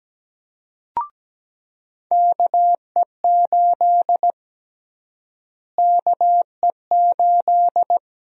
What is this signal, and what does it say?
Morse code, heard as a keyed tone, about 17 wpm, 700 Hz. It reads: KE8 KE8
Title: USA Call Sign Prefixes - Repeated 2x - 17wpm